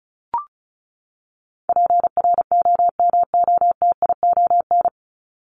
Morse code, heard as a keyed tone, about 34 wpm, 700 Hz. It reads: PROMOTION
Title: Top 1501-1750 Words 35wpm